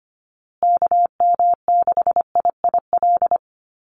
Morse code, heard as a keyed tone, about 25 wpm, 700 Hz. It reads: KM6IIL
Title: USA Call Signs Repeated 2x 25wpm